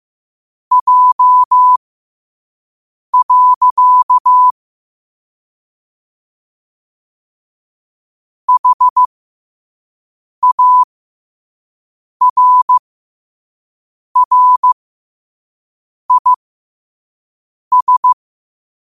Morse code, heard as a keyed tone, about 15 wpm, spaced as for 5 wpm, 1000 Hz. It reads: J. HARRIS
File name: qotdfiles / 5wpm